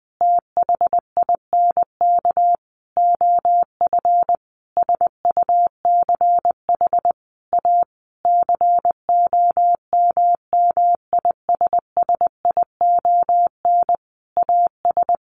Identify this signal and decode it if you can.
Morse code, heard as a keyed tone, about 20 wpm, 700 Hz. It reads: THINK OF SUCH A COMMISSION AS
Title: Sentences from Top 1400 Words 20wpm